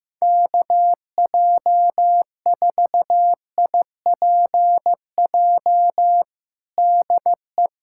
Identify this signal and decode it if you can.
Morse code, heard as a keyed tone, about 15 wpm, 700 Hz. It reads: KJ4IPJ DE
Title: USA Callsign DE Callsign - 6 Characters - 15wpm